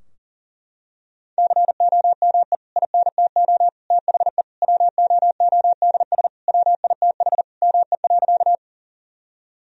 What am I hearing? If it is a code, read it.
Morse code, heard as a keyed tone, about 40 wpm, 700 Hz. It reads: COME INTO THE WOODS WITH ME.
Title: Sentences from Top 1000 Words - Review Part 2 - 40wpm